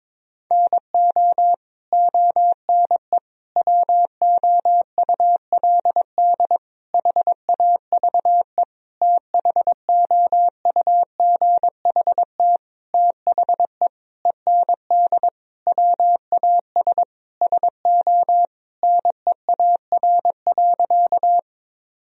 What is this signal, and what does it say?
Morse code, heard as a keyed tone, about 22 wpm, 700 Hz. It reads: NO ONE WOULD HAVE THOUGHT THE END WAS SO NEAR.
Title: Sentences from Top 1000 Words - Review Part 1 - 22wpm